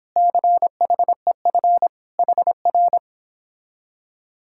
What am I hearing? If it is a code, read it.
Morse code, heard as a keyed tone, about 26 wpm, 700 Hz. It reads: CHEF HR
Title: Speed-Racing Common QSO Phrases Encore - 22wpm